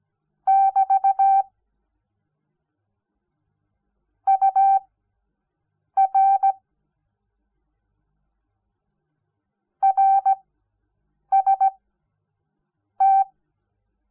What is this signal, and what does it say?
Morse code, heard as a keyed tone, about 17 wpm, spaced as for 6 wpm, 780 Hz. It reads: <BT> UR RST
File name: Onlinelehrgang / morsen / DJ4UF Morsen06